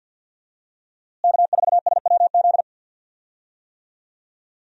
Morse code, heard as a keyed tone, about 47 wpm, 700 Hz. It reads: K4SWZ
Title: USA Call Signs - 5 Characters - Repeated 2x - 50wpm